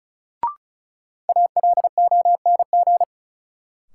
Morse code, heard as a keyed tone, about 34 wpm, 700 Hz. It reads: ALONG
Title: Sentences from Top 700 Words - New Word Drill - 35wpm